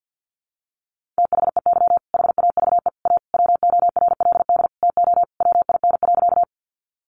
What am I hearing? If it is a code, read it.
Morse code, heard as a keyed tone, about 50 wpm, 700 Hz. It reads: THEY HAVE A WORLD TO WIN.
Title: Sentences from Top 1500 Words 50wpm